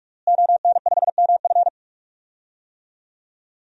Morse code, heard as a keyed tone, about 46 wpm, 700 Hz. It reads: ON5MF
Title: DXCC-100 Call Signs - 5 Characters - Repeated 3x - 45wpm